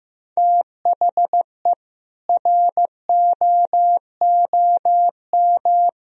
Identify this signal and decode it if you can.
Morse code, heard as a keyed tone, about 15 wpm, 700 Hz. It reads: THE ROOM
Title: Sentences from Top 1000 Words - Review Part 2 - 15wpm